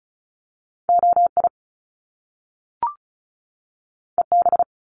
Morse code, heard as a keyed tone, about 36 wpm, 700 Hz. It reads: OI EB
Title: Top Two Letter Combinations in Words 35wpm